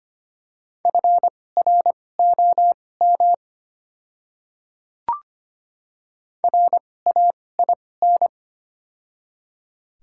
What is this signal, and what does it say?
Morse code, heard as a keyed tone, about 25 wpm, spaced as for 18 wpm, 700 Hz. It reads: FROM RAIN
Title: 2x Character Spacing - Top 500 Words - 25wpm